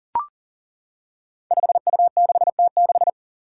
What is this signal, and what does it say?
Morse code, heard as a keyed tone, about 39 wpm, 700 Hz. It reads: 5U6T6
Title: Five Challenging Letter-Number Combinations 40wpm